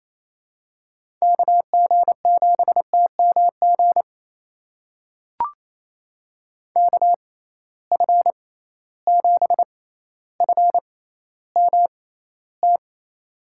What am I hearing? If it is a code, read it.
Morse code, heard as a keyed tone, about 28 wpm, spaced as for 14 wpm, 700 Hz. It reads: KG7TMG KF7FMT
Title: USA Call Signs - 6x Character Spacing - 28wpm